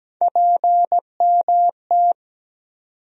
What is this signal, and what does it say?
Morse code, heard as a keyed tone, about 17 wpm, 700 Hz. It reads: PMT
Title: USA Call Sign Postfixes - 3 Letters 17wpm